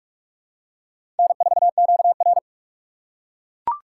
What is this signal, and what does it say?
Morse code, heard as a keyed tone, about 41 wpm, 700 Hz. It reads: N4QR
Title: USA Call Signs - 4 Characters - 3x Character Spacing - 45wpm